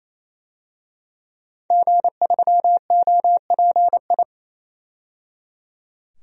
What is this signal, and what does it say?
Morse code, heard as a keyed tone, about 28 wpm, 700 Hz. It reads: G3OPI
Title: DXCC-100 Call Signs - 5 Characters - 28wpm